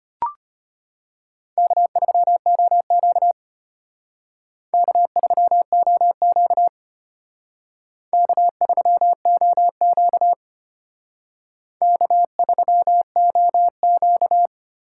Morse code, heard as a keyed tone, about 32 wpm, 700 Hz. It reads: K3OQ K3OQ K3OQ K3OQ